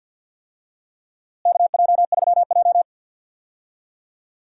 Morse code, heard as a keyed tone, about 46 wpm, 700 Hz. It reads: KJ3J
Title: USA Call Signs - 4 Characters - Repeated 3x - 50wpm